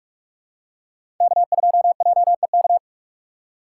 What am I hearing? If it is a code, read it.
Morse code, heard as a keyed tone, about 41 wpm, 700 Hz. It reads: K2JEK
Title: USA Call Signs - 6x Character Spacing - 45wpm